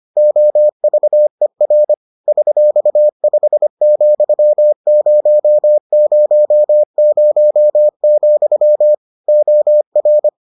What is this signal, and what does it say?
Morse code, heard as a keyed tone, about 25 wpm, 600 Hz. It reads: OVER $5,000, OR